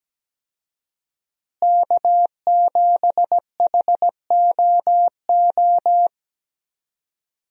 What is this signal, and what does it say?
Morse code, heard as a keyed tone, about 17 wpm, 700 Hz. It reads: K7HOO
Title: USA Call Signs - 5 Characters - Repeated 3x - 17wpm